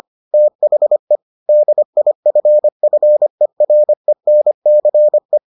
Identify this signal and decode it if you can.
Morse code, heard as a keyed tone, about 25 wpm, 600 Hz. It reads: THE DIFFERENCE